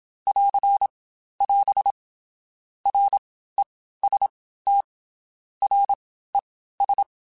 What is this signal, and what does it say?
Morse code, heard as a keyed tone, about 27 wpm, 800 Hz. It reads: <AR> <AS> R E S T R E S
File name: Basic Letter Groups By7Groups / VB46 / VB46@25